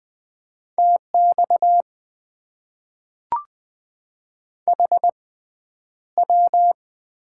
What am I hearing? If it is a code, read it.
Morse code, heard as a keyed tone, about 20 wpm, spaced as for 10 wpm, 700 Hz. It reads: TX HW
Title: Top 90 QSO Elements - 6x Character Spacing - 20wpm